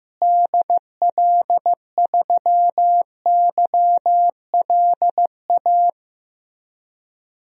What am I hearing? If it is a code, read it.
Morse code, heard as a keyed tone, about 15 wpm, 700 Hz. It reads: DL3YLA
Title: DXCC-100 Call Signs - 6 Characters - Repeated 2x - 15wpm